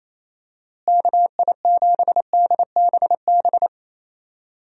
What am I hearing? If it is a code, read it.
Morse code, heard as a keyed tone, about 28 wpm, 700 Hz. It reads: KI7DBB
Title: USA Call Signs - 6 Characters - Repeated 2x - 28wpm